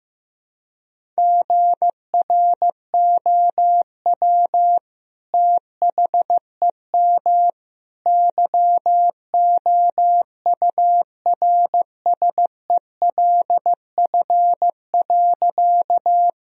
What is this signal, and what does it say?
Morse code, heard as a keyed tone, about 15 wpm, 700 Hz. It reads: GROW THEM YOURSELF.
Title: Sentences from Top 1000 Words - Review Part 1 - 15wpm